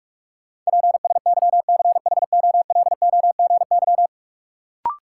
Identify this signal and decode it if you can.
Morse code, heard as a keyed tone, about 46 wpm, 700 Hz. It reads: PSYCHOLOGY
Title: Sentences from Top 1400 Words - New Word Drill - 45wpm